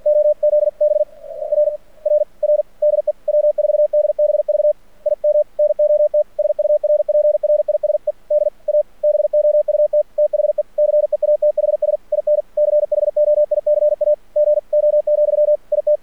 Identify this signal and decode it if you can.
Morse code, heard as a keyed tone, about 46 wpm, 600 Hz. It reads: OZB RRR DE W4DBV IM NOT SURPRISED ABOUT THE CEATHS IN CHOICAGO, IT
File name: archived / CRSnet